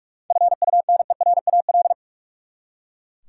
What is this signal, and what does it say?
Morse code, heard as a keyed tone, about 43 wpm, 700 Hz. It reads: FUNERAL
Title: Seven Letter Words - 45wpm